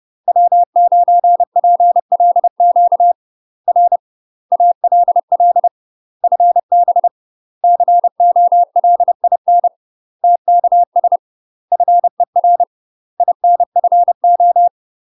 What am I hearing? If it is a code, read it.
Morse code, heard as a keyed tone, about 30 wpm, 700 Hz. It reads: W9PLQ R ALL FB COLIN TKS FER INFO
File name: UK MASTER QSOs TRAINING / 30WPM / UK MASTER QSOs 0042